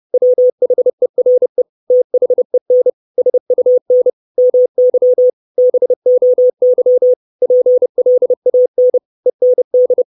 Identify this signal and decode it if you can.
Morse code, heard as a keyed tone, about 30 wpm, 500 Hz. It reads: WHERE THEN SUN MY BOY PLAN END
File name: 632 Words 30WPM